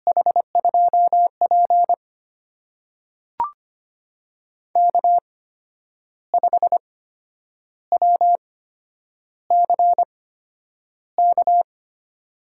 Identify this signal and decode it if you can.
Morse code, heard as a keyed tone, about 25 wpm, spaced as for 9 wpm, 700 Hz. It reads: H2P K5WCK
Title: USA Call Signs - 8x Character Spacing - 25wpm